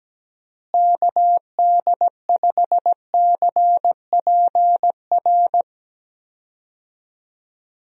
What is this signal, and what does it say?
Morse code, heard as a keyed tone, about 17 wpm, 700 Hz. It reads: KD5CPR
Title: USA Call Signs - 6 Characters - Repeated 2x - 17wpm